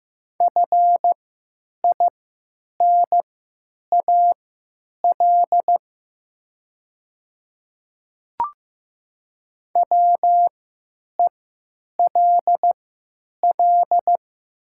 Morse code, heard as a keyed tone, about 15 wpm, spaced as for 9 wpm, 700 Hz. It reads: FINAL WELL
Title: 3x Character Spacing - Top 500 Words - 15wpm